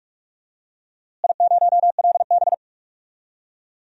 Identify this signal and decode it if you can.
Morse code, heard as a keyed tone, about 46 wpm, 700 Hz. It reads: I0LB
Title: DXCC-100 Call Signs - 4 Characters - Repeated 3x - 45wpm